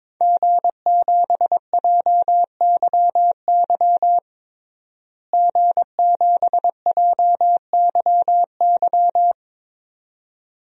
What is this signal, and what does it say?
Morse code, heard as a keyed tone, about 22 wpm, 700 Hz. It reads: G7JYY G7JYY
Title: DXCC-100 Call Signs - 5 Characters - Repeated 2x - 22wpm